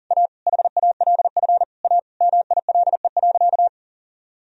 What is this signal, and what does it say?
Morse code, heard as a keyed tone, about 40 wpm, 700 Hz. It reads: A HALF A MILE.
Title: Sentences from Top 1000 Words - Review Part 1 - 40wpm